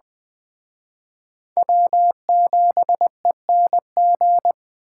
Morse code, heard as a keyed tone, about 21 wpm, 700 Hz. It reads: W7ENG